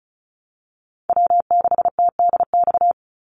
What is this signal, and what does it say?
Morse code, heard as a keyed tone, about 34 wpm, 700 Hz. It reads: W6TDX
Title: NA SOTA Chaser Callsigns 35wpm